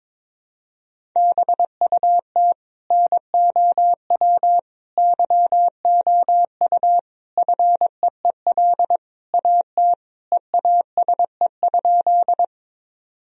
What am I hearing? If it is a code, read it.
Morse code, heard as a keyed tone, about 22 wpm, 700 Hz. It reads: BUT NOW YOU FEEL AT EASE?
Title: Sentences from Top 1000 Words - Review Part 1 - 22wpm